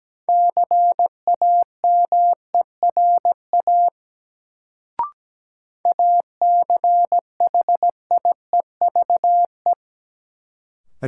Morse code, heard as a keyed tone, about 17 wpm, 700 Hz. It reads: CAMERA ACHIEVE
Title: Sentences from Top 1000 Words - Review Part 5 - New Word Drill - 17wpm